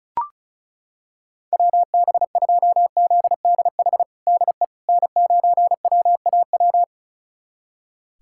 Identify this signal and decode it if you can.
Morse code, heard as a keyed tone, about 34 wpm, 700 Hz. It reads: WB2ZDH DE N9WAW
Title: USA Callsign DE Callsign 35wpm